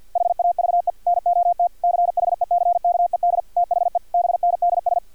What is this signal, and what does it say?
Morse code, heard as a keyed tone, about 50 wpm, 700 Hz, with radio noise. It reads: HAVE NOT CHECKED THE BNDS
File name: archived / CRSnet